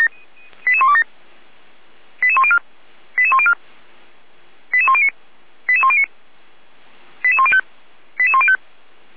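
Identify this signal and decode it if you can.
Five-tone selective call (ZVEI), heard as quick runs of five tones, tones of about 70 ms. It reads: 88127 88104 88104 88109 88109 88105 88105
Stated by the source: ZVEI five-tone selective call received by SDR, archived by Feuerwehr Orlamünde (fire brigade).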